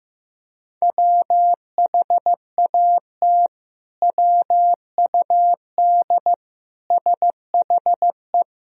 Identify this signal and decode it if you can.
Morse code, heard as a keyed tone, about 15 wpm, 700 Hz. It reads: WHAT WUD SHE
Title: Sentences from Top 100 Words and Abbreviations 15wpm